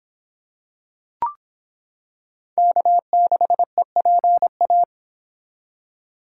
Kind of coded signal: Morse code

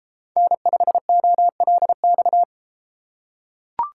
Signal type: Morse code

